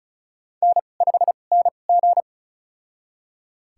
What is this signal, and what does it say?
Morse code, heard as a keyed tone, about 36 wpm, spaced as for 25 wpm, 700 Hz. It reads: N5NG
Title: USA Call Signs - 4 Characters - 2x Character Spacing - 35wpm